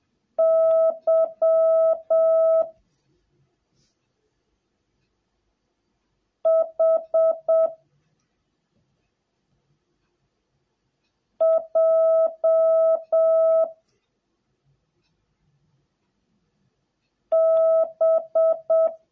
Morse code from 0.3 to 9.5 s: Y H